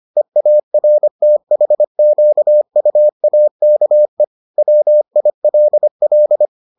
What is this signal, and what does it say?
Morse code, heard as a keyed tone, about 25 wpm, 600 Hz. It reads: EARTHQUAKE WILL